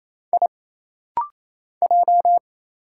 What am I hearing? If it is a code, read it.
Morse code, heard as a keyed tone, about 28 wpm, 700 Hz. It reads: I J
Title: Single Letter-Number Alt - Rapid-Fire 28wpm